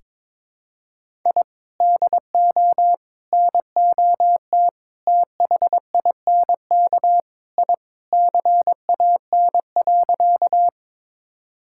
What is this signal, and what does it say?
Morse code, heard as a keyed tone, about 22 wpm, 700 Hz. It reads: I DO NOT THINK I CAN.
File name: practice / 22wpm / Sentences with 4-Word Phrases - 22wpm